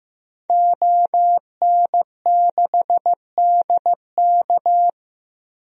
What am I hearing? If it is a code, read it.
Morse code, heard as a keyed tone, about 16 wpm, 700 Hz. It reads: ON6DK
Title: DXCC-100 Call Signs - 5 Characters - Repeated 3x - 15wpm